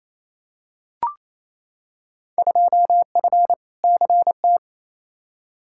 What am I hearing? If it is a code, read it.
Morse code, heard as a keyed tone, about 28 wpm, 700 Hz. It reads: 2F CT
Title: Field Day Exchanges Encore - 28wpm